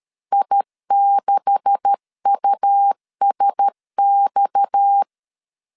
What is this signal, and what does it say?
Morse code, heard as a keyed tone, about 13 wpm, 800 Hz. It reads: I6USX
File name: LEZIONI / 9 286 12wpm 2305